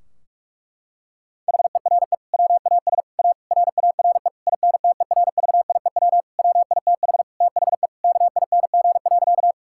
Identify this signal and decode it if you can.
Morse code, heard as a keyed tone, about 46 wpm, 700 Hz. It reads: HERE WAS A RARE INTERVIEW WITH THE KING.
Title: Sentences from Top 1600 Words 45wpm